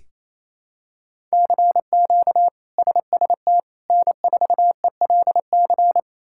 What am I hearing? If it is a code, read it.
Morse code, heard as a keyed tone, about 28 wpm, 700 Hz. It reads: CQ SST N4ELC
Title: K1USN SST QSO Practice - 28wpm